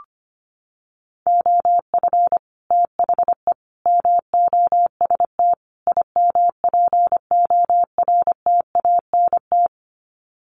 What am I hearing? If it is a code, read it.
Morse code, heard as a keyed tone, about 25 wpm, 700 Hz. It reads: OF THE MOST IMPORTANT
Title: Sets of 4 Words 25wpm